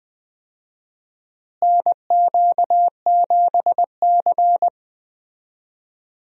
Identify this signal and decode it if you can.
Morse code, heard as a keyed tone, about 20 wpm, 700 Hz. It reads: NQ7C
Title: USA Call Signs - 4 Characters - Repeated 3x - 20wpm